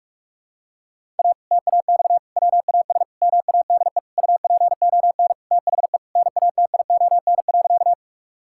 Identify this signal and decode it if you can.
Morse code, heard as a keyed tone, about 46 wpm, 700 Hz. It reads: A TAX WAS MADE UPON THE NATION.
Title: Sentences from Top 1000 Words - Review Part 6 - 45wpm